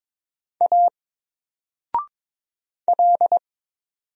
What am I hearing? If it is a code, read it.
Morse code, heard as a keyed tone, about 22 wpm, 700 Hz. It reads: A L
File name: Single-Letter-Number-22wpm